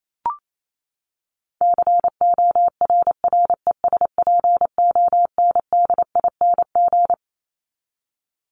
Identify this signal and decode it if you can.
Morse code, heard as a keyed tone, about 28 wpm, 700 Hz. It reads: CORRESPONDING
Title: ING Suffix - Top 100 Words with Base Words 28wpm